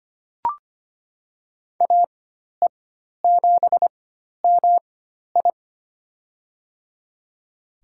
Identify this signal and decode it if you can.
Morse code, heard as a keyed tone, about 25 wpm, spaced as for 12 wpm, 700 Hz. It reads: AE7MI
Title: USA Call Signs - 5 Characters - 4x Character Spacing - 25wpm